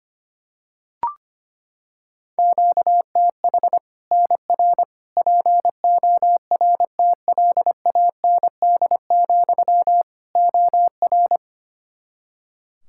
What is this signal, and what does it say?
Morse code, heard as a keyed tone, about 25 wpm, 700 Hz. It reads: QTH NR PORTLAND, OR
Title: Practice QSO - Exchange 1 - Beginning - 25wpm